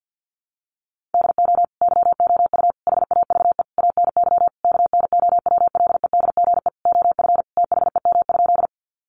Morse code, heard as a keyed tone, about 50 wpm, 700 Hz. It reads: DO YOU HAVE ANY KNOWLEDGE OF THEM?
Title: Sentences from Top 600 Words 50wpm